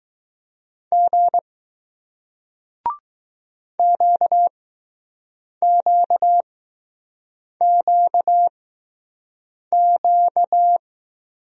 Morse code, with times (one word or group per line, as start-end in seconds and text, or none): 0.9-1.4 G
3.8-10.8 QQQQ